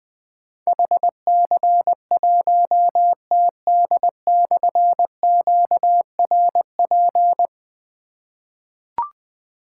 Morse code, heard as a keyed tone, about 20 wpm, 700 Hz. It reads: HC1TD/QRP
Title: Difficult Call Signs 20wpm